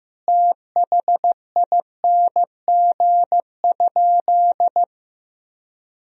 Morse code, heard as a keyed tone, about 15 wpm, 700 Hz. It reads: THING?
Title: Sentences from Top 200 Words and Abbreviations 15wpm